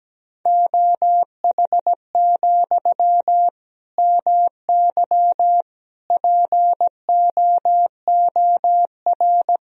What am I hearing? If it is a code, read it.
Morse code, heard as a keyed tone, about 17 wpm, 700 Hz. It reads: OH, MY POOR
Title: Sentences from Top 1000 Words - Review Part 3 - 17wpm